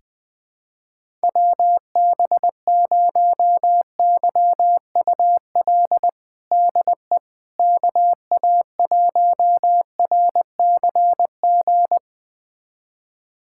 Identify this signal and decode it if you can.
Morse code, heard as a keyed tone, about 20 wpm, 700 Hz. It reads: WB0YUL DE KA1RCG
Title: USA Callsign DE Callsign 20wpm